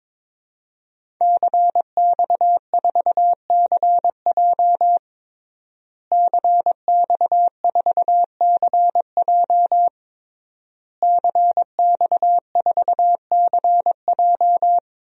Morse code, heard as a keyed tone, about 22 wpm, 700 Hz. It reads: CX4CJ CX4CJ CX4CJ
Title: DXCC-100 Call Signs - 5 Characters - Repeated 3x - 22wpm